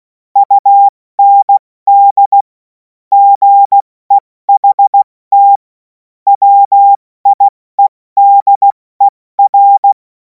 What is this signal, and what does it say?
Morse code, heard as a keyed tone, about 16 wpm, 800 Hz. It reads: UND GEHT WIEDER
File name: Morsekurs / Witze / Witz 17. März 2026 70 F80